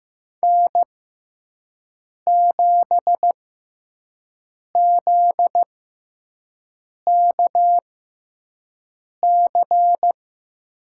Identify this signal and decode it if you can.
Morse code, heard as a keyed tone, about 15 wpm, spaced as for 5 wpm, 700 Hz. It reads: N7ZKC
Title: USA Call Signs - 5 Characters - 6x Character Spacing - 15wpm